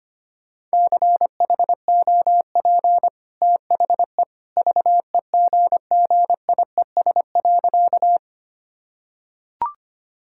Morse code, heard as a keyed tone, about 25 wpm, 690 Hz. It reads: CHOP THE VEGGIES.
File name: Easy Sentences with Hints - 25wpm